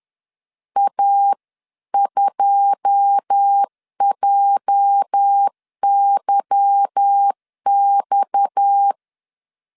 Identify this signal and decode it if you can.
Morse code, heard as a keyed tone, about 10 wpm, 800 Hz. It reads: A2JYX
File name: CW / DET / 1 9 1500C S A 10wpm 4511